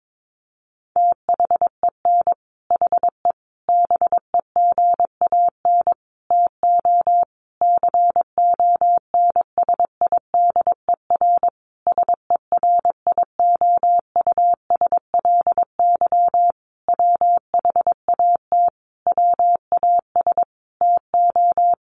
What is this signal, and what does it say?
Morse code, heard as a keyed tone, about 22 wpm, 700 Hz. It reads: THEN HE BEGAN TO CONSIDER SERIOUSLY WHAT WAS TO
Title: Sentences from Top 1200 Words 22wpm